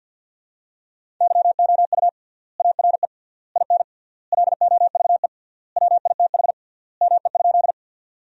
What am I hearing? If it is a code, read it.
Morse code, heard as a keyed tone, about 50 wpm, 700 Hz. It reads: YOU ARE IN LOVE WITH ME?
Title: 3x Word-Spacing - Sentences from Top 400 Words - 50wpm